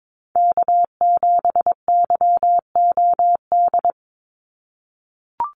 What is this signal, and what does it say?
Morse code, heard as a keyed tone, about 22 wpm, 700 Hz. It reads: K7YOD
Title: USA Call Signs 22wpm